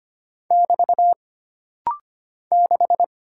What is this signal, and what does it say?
Morse code, heard as a keyed tone, about 25 wpm, 700 Hz. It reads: <BT> 6
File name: Single Character (Letter:Number:Punctuation:Prosign) - Rapid-Fire - 25wpm